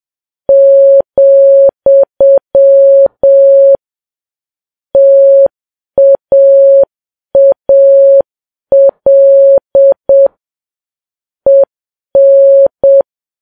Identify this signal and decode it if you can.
Morse code, heard as a keyed tone, about 7 wpm, 550 Hz. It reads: , TAAL EN
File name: Files Web tls / Morse on9fv / morse7kps